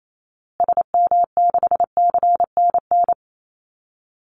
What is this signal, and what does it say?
Morse code, heard as a keyed tone, about 28 wpm, 700 Hz. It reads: SM6CNN
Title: Call Signs Repeated 3x - 28wpm